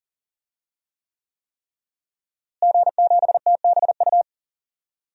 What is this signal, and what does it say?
Morse code, heard as a keyed tone, about 40 wpm, 700 Hz. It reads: G7TBU
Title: DXCC-100 Call Signs - 5 Characters - Repeated 3x - 40wpm